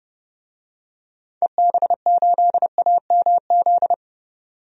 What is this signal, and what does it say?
Morse code, heard as a keyed tone, about 30 wpm, 700 Hz. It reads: EB8AMZ
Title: DXCC-100 Call Signs - 6 Characters - 30wpm